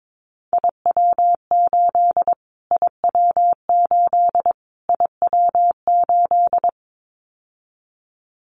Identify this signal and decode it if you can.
Morse code, heard as a keyed tone, about 22 wpm, 700 Hz. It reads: IW8 IW8 IW8
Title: Callsign Prefixes for Top 25 DXCC - Repeated 3x - 22wpm